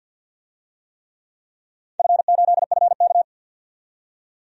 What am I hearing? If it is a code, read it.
Morse code, heard as a keyed tone, about 51 wpm, 700 Hz. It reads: F8FK
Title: DXCC-100 Call Signs - 4 Characters - 50wpm